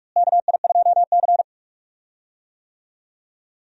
Morse code, heard as a keyed tone, about 42 wpm, 700 Hz. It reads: KI2C
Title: USA Call Signs - 4 Characters - Repeated 2x - 45wpm